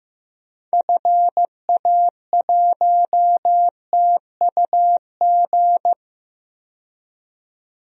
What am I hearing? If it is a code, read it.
Morse code, heard as a keyed tone, about 15 wpm, 700 Hz. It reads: FA1TUG
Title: DXCC-100 Call Signs - 6 Characters - 15wpm